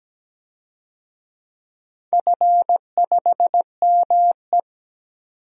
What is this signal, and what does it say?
Morse code, heard as a keyed tone, about 17 wpm, 700 Hz. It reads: F5ME
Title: DXCC-100 Call Signs - 4 Characters - 17wpm